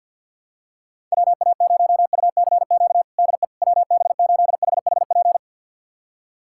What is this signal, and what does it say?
Morse code, heard as a keyed tone, about 52 wpm, 700 Hz. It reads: WA0UCQ DE WB8HHP
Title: USA Callsign DE Callsign - 6 Characters - 50wpm